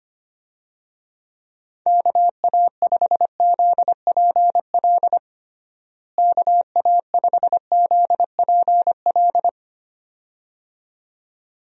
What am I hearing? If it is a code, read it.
Morse code, heard as a keyed tone, about 25 wpm, 700 Hz. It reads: KA5ZPL KA5ZPL
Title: USA Call Signs - 6 Characters - Repeated 2x - 25wpm